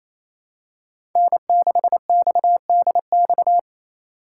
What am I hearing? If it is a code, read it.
Morse code, heard as a keyed tone, about 28 wpm, 700 Hz. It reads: N6XDX
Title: USA Call Signs - 2x Character Spacing - 28wpm